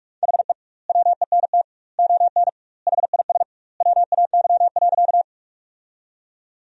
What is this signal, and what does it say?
Morse code, heard as a keyed tone, about 46 wpm, 700 Hz. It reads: HE WENT ON HIS WAY.
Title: 2x Word-Spacing - Sentences from Top 200 Words - 45wpm